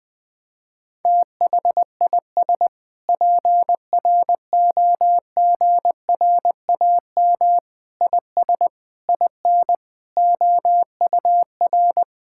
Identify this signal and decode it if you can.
Morse code, heard as a keyed tone, about 21 wpm, 700 Hz. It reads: THIS PROGRAM IS IN OUR
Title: Sentences from Top 1000 Words - Review Part 5 - 20wpm